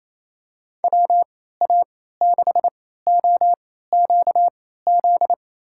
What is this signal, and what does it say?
Morse code, heard as a keyed tone, about 28 wpm, spaced as for 16 wpm, 700 Hz. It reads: WA6OQZ
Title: USA Call Signs - 6 Characters - 3x Character Spacing - 28wpm